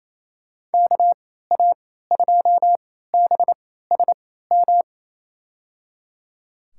Morse code, heard as a keyed tone, about 28 wpm, spaced as for 16 wpm, 700 Hz. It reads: KA2BSM
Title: USA Call Signs - 6 Characters - 3x Character Spacing - 28wpm